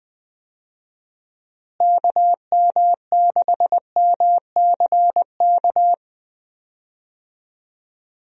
Morse code, heard as a keyed tone, about 19 wpm, 700 Hz. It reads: KM6MCK
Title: USA Call Signs - 6 Characters - Repeated 2x - 20wpm